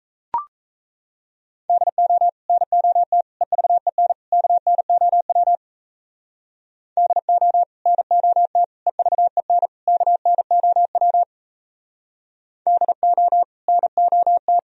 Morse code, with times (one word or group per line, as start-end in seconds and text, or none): 1.7-2.3 DO
2.5-3.2 NOT
3.4-4.1 EVEN
4.3-5.6 KNOW
7.0-7.6 DO
7.9-8.6 NOT
8.9-9.7 EVEN
9.9-11.2 KNOW
12.7-13.4 DO
13.7-14.6 NOT